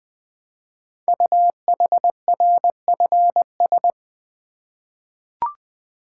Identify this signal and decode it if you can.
Morse code, heard as a keyed tone, about 20 wpm, 700 Hz. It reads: UHRFS
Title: Code Groups - 5 Letters - 20wpm